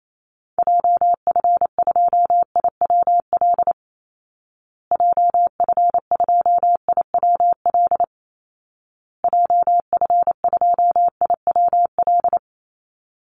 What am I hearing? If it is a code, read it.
Morse code, heard as a keyed tone, about 28 wpm, 700 Hz. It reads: JF2IWL JF2IWL JF2IWL
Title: Call Signs Repeated 2x - 28wpm